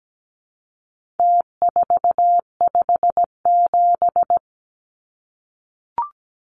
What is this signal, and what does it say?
Morse code, heard as a keyed tone, about 17 wpm, 700 Hz. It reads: T457
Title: Numbers - Contest Style - 4-Digit 17wpm